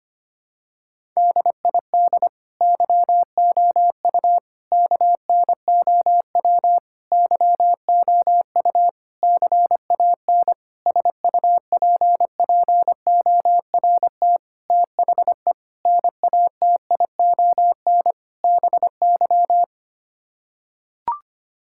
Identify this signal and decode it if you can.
Morse code, heard as a keyed tone, about 25 wpm, 700 Hz. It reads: DID YOU KNOW YOU CAN SUPPORT THE NATION BY
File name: Sets of 9 Words - 25wpm